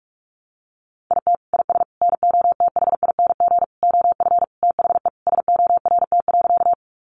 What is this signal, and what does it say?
Morse code, heard as a keyed tone, about 44 wpm, 700 Hz. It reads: IT IS NOTHING OF THE SORT.
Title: Sentences from Top 1300 Words 45wpm